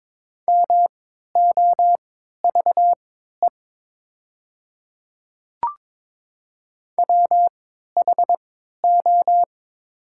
Morse code, heard as a keyed tone, about 22 wpm, spaced as for 12 wpm, 700 Hz. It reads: MOVE WHO
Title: 3x Character Spacing - Top 200 Words - 22wpm